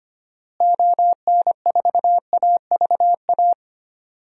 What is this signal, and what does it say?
Morse code, heard as a keyed tone, about 25 wpm, 700 Hz. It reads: ON4AVA
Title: DXCC-100 Call Signs - 6 Characters - Repeated 3x - 25wpm